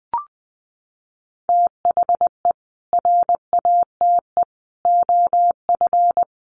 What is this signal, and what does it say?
Morse code, heard as a keyed tone, about 20 wpm, 700 Hz. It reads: THE RATE OF
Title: Sentences from Top 1100 Words 20wpm